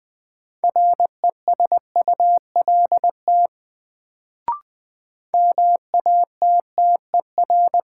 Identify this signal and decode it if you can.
Morse code, heard as a keyed tone, about 20 wpm, 700 Hz. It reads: RESULT MATTER
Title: Sentences from Top 700 Words - New Word Drill - 20wpm